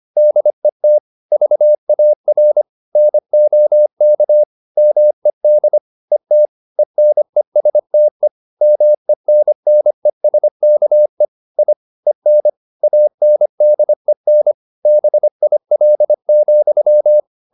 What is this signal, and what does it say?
Morse code, heard as a keyed tone, about 25 wpm, 600 Hz. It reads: DET VAR NOK MED ET ENESTE MENNESKE I EN ANDEN BIL,
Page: Ferie 25 wpm